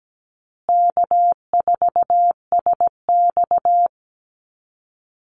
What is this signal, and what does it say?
Morse code, heard as a keyed tone, about 17 wpm, 700 Hz. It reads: K4SX
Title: Easy Call Signs Repeated 2x - 17wpm